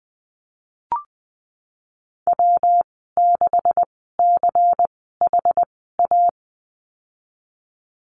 Morse code, heard as a keyed tone, about 20 wpm, spaced as for 15 wpm, 700 Hz. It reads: W6CHA
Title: 186 Lesson 30g - Letter b - Call Signs 20wpm